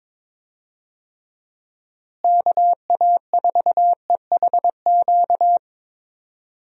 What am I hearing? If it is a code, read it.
Morse code, heard as a keyed tone, about 22 wpm, 700 Hz. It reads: KA4EHQ
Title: USA Call Signs - 6 Characters - Repeated 3x - 22wpm